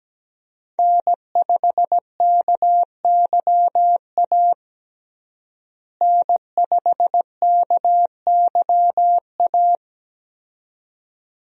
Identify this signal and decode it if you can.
Morse code, heard as a keyed tone, about 17 wpm, 700 Hz. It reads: N5KYA N5KYA
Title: USA Call Signs - 5 Characters - Repeated 2x - 17wpm